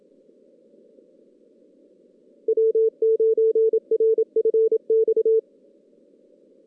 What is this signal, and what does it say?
Morse code, heard as a keyed tone, about 24 wpm, 440 Hz. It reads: W9RFX
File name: 2016 JAN CW NAQP / 2016 Jan naqp k5ot - 0-6